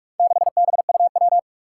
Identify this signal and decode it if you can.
Morse code, heard as a keyed tone, about 44 wpm, 700 Hz. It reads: 6BUW
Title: Four Challenging Letter-Number Combinations 45wpm